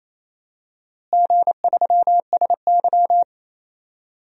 Morse code, heard as a keyed tone, about 28 wpm, 700 Hz. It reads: G3SY
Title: DXCC-100 Call Signs - 4 Characters - Repeated 3x - 28wpm